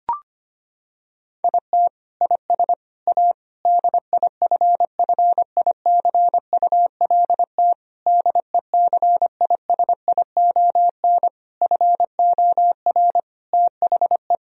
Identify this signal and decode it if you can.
Morse code, heard as a keyed tone, about 25 wpm, 700 Hz. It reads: IT IS A DIFFICULT DECISION FOR THE
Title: Sentences from Top 1000 Words - Review Part 4 - 25wpm